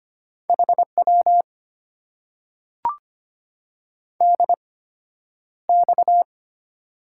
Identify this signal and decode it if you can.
Morse code, heard as a keyed tone, about 25 wpm, spaced as for 11 wpm, 700 Hz. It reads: HW DX
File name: Top 90 QSO Elements - 8x Character Spacing - 25wpm